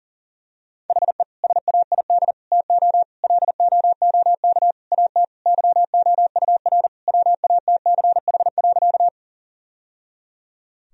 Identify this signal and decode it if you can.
Morse code, heard as a keyed tone, about 40 wpm, 700 Hz. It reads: HE SAID TO LOOK AT YOUR WATCH.
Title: Sentences from Top 1000 Words - Review Part 1 - 40wpm